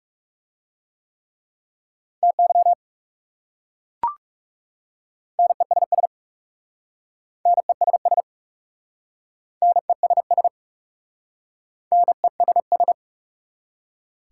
Morse code, heard as a keyed tone, about 40 wpm, 690 Hz. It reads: TY NESS NESS NESS NESS